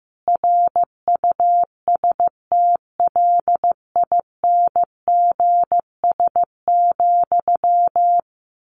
Morse code, heard as a keyed tone, about 15 wpm, 700 Hz. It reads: RUSTLINGS,
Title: A Sound Of Thunder 15wpm